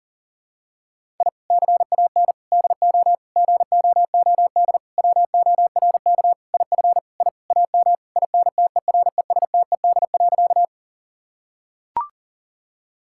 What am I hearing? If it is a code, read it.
Morse code, heard as a keyed tone, about 40 wpm, 700 Hz. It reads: I CAN DO GOOD WORK IF I AM INTERESTED.
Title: Sentences from Top 1000 Words - Review Part 6 - 40wpm